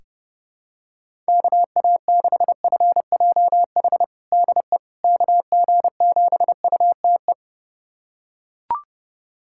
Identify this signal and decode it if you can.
Morse code, heard as a keyed tone, about 30 wpm, 700 Hz. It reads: KA6FJH DE KG7UTE
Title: USA Callsign DE Callsign - 6 Characters - 30wpm